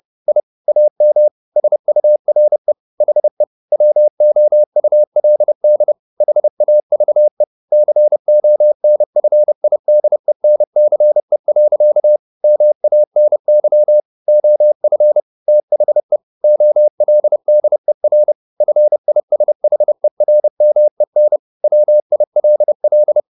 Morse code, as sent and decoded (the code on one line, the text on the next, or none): .. / .- -- / ... ..- .-. / .... / .-- --- ..- .-.. -.. / .... .- ...- . / -.-. --- -. ..-. .. -.. -. -.-. .-.-.- / -- .- -. -.-- / --- ..-. / - .... / --- .-.. -.. .-. / ..-. .. ... .... .-. -- . -. / .-- .. .-.. .-..
I AM SURE HE WOULD HAVE CONFIDENCE. MANY OF THE OLDER FISHERMEN WILL